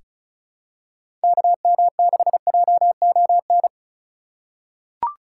Morse code, heard as a keyed tone, about 34 wpm, 700 Hz. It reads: KM6JON
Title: USA Call Signs - 6 Characters - 4x Character Spacing - 35wpm